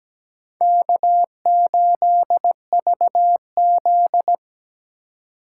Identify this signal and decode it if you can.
Morse code, heard as a keyed tone, about 17 wpm, 700 Hz. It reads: K8VZ
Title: USA Call Signs - 4 Characters - 3x Character Spacing - 17wpm